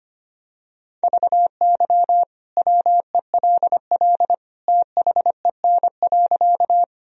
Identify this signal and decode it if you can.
Morse code, heard as a keyed tone, about 25 wpm, 700 Hz. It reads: VY WELL THEN.
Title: Sentences from Top 200 Words and Abbreviations 25wpm